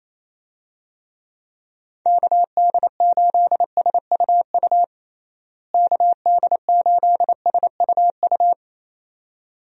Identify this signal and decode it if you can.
Morse code, heard as a keyed tone, about 28 wpm, 700 Hz. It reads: KD8SUU KD8SUU
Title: USA Call Signs - 6 Characters - Repeated 2x - 28wpm